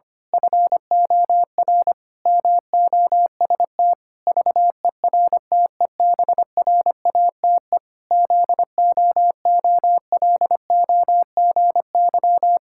Morse code, heard as a keyed tone, about 25 wpm, 700 Hz. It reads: FOR MOST VERTEBRATE ZOOLOGY
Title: Pangram Challenge v2 - 25wpm